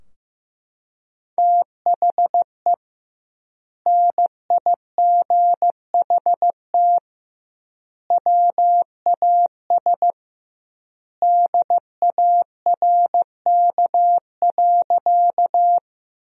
Morse code, heard as a keyed tone, about 15 wpm, 700 Hz. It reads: THE NIGHT WAS DARK.
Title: 2x Word-Spacing - Sentences from Top 500 Words - 15wpm